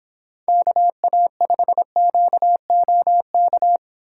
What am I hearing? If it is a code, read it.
Morse code, heard as a keyed tone, about 30 wpm, 700 Hz. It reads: KA5QOK